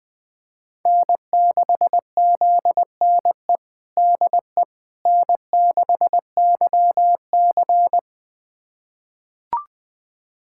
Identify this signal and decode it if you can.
Morse code, heard as a keyed tone, about 20 wpm, 700 Hz. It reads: N6ZNE DE N6YC
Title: USA Callsign DE Callsign 20wpm